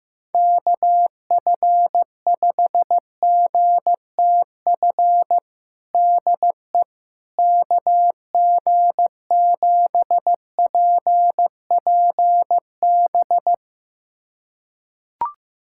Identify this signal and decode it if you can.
Morse code, heard as a keyed tone, about 15 wpm, 700 Hz. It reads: KF5GTF DE KG7PPB
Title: USA Callsign DE Callsign 15wpm